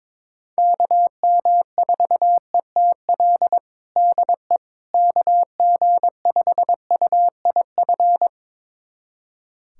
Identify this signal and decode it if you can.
Morse code, heard as a keyed tone, about 22 wpm, 700 Hz. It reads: KM4ETL DE KG5UIF
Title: USA Callsign DE Callsign - 6 Characters - 22wpm